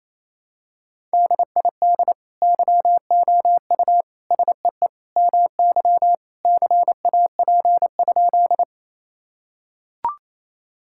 Morse code, heard as a keyed tone, about 28 wpm, 700 Hz. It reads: DID YOU SEE MY CAP?
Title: Sentences from Top 1600 Words 28wpm